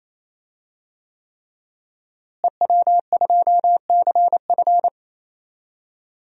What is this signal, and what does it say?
Morse code, heard as a keyed tone, about 28 wpm, 700 Hz. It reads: EW2CF
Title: DXCC-100 Call Signs - 5 Characters - Repeated 3x - 28wpm